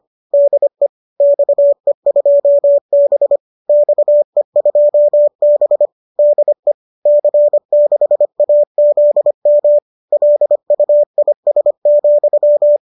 Morse code, heard as a keyed tone, about 25 wpm, 590 Hz. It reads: DE XE2B XE2B DE C6AZM LUIS,